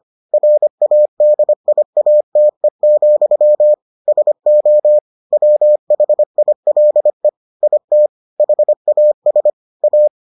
Morse code, heard as a keyed tone, about 25 wpm, 600 Hz. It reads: RADIATE, SO WHILE IT HAS A